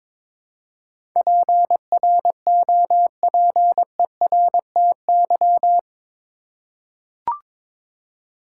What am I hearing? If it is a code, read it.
Morse code, heard as a keyed tone, about 22 wpm, 700 Hz. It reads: PROPERTY
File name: practice / 22wpm / Eight Letter Words - 22wpm